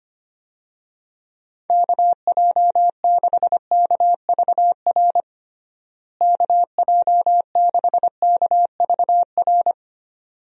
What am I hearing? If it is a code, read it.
Morse code, heard as a keyed tone, about 25 wpm, 700 Hz. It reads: KJ6KVR KJ6KVR
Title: USA Call Signs - 6 Characters - Repeated 2x - 25wpm